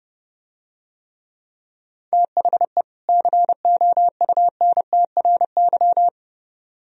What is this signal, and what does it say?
Morse code, heard as a keyed tone, about 29 wpm, 700 Hz. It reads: THE COUNTRY